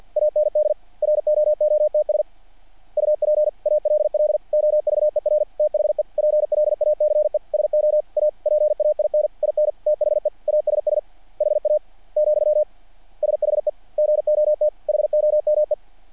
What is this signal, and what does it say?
Morse code, heard as a keyed tone, about 50 wpm, 600 Hz. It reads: AND ROOTS UP ALL OVER THE PLACE SO A PAIN IN THE ASS HA , SHE GOT HOME
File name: archived / CRSnet